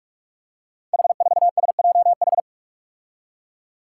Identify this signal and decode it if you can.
Morse code, heard as a keyed tone, about 44 wpm, 700 Hz. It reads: H4SJH
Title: Five Challenging Letter-Number Combinations 45wpm